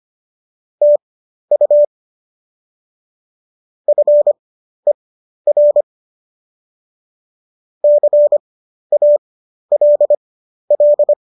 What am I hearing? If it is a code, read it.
Morse code, heard as a keyed tone, about 25 wpm, spaced as for 12 wpm, 590 Hz. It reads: TU FER CALL
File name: qso012 12